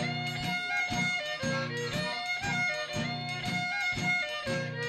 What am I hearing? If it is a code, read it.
Emergency Alert System (SAME) tones, heard as harsh buzzing bursts. It is heard from 0.5 to 4.3 s.